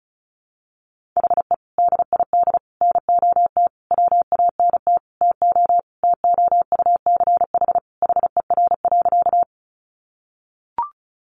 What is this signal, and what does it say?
Morse code, heard as a keyed tone, about 34 wpm, 700 Hz. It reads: HE DID NOT WANT TO TOUCH HER.
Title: Sentences from Top 1200 Words 35wpm